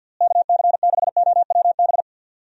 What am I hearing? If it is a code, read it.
Morse code, heard as a keyed tone, about 51 wpm, 700 Hz. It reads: KC6OWB
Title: USA Call Signs - 6 Characters - Repeated 3x - 50wpm